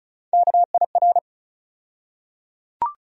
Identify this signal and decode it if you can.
Morse code, heard as a keyed tone, about 34 wpm, 700 Hz. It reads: KIR
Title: USA Call Sign Postfixes - 3 Letters 35wpm